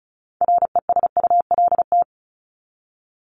Morse code, heard as a keyed tone, about 34 wpm, 700 Hz. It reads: RESULT
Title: ING Suffix - Top 100 Words with Base Words 35wpm